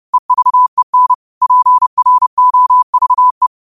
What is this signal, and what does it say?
Morse code, heard as a keyed tone, about 30 wpm, 1000 Hz. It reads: EVEN PROVE
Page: New quotes every day in morse code at 30 Words per minute.